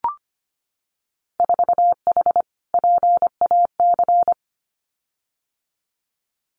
Morse code, heard as a keyed tone, about 25 wpm, 700 Hz. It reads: 4H PAC
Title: Winter Field Day Exchanges Encore 25wpm